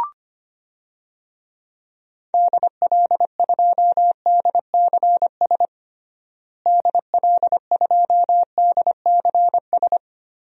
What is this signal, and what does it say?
Morse code, heard as a keyed tone, about 25 wpm, 700 Hz. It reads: DL2DCS DL2DCS
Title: DXCC-100 Call Signs - 6 Characters - Repeated 3x - 25wpm